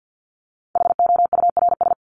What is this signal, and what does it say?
Morse code, heard as a keyed tone, about 50 wpm, 700 Hz. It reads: HOURS
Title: ES:S Suffix - Top 100 Words with Base Words 50wpm